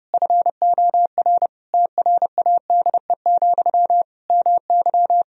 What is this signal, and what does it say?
Morse code, heard as a keyed tone, about 30 wpm, 700 Hz. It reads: FOR TRADE, MY
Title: Sentences from Top 1000 Words - Review Part 4 - 30wpm